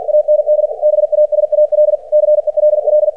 Morse code, heard as a keyed tone, about 50 wpm, 600 Hz. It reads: AND BASAL CELL
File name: archived / CRSnet